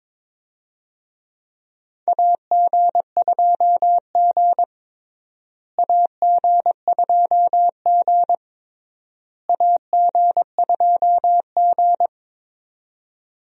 Morse code, heard as a keyed tone, about 22 wpm, 700 Hz. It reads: AG2G AG2G AG2G
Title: USA Call Signs - 4 Characters - Repeated 3x - 22wpm